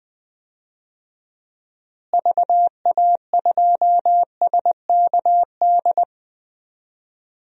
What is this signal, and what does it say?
Morse code, heard as a keyed tone, about 20 wpm, 700 Hz. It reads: VA2SKD
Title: DXCC-100 Call Signs - 6 Characters - Repeated 2x - 20wpm